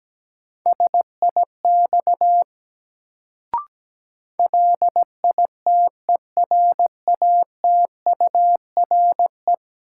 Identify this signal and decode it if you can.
Morse code, heard as a keyed tone, about 17 wpm, 700 Hz. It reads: SIX LITERATURE
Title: Sentences from Top 600 Words - New Word Drill - 17wpm